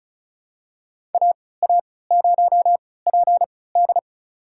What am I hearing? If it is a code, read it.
Morse code, heard as a keyed tone, about 36 wpm, spaced as for 20 wpm, 700 Hz. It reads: AA0PD
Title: USA Call Signs - 5 Characters - 3x Character Spacing - 35wpm